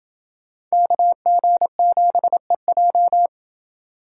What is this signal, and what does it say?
Morse code, heard as a keyed tone, about 29 wpm, 700 Hz. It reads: KG7EJ